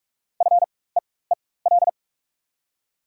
Morse code, heard as a keyed tone, about 46 wpm, spaced as for 21 wpm, 690 Hz. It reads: FEEL
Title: 4x Character Spacing - Top 500 Words - 45wpm